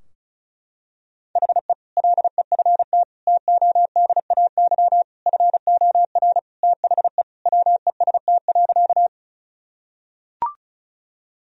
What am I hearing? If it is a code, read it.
Morse code, heard as a keyed tone, about 34 wpm, 700 Hz. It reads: HE LEFT TODAY FOR THE WEST.
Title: Sentences from Top 1000 Words - Review Part 2 - 35wpm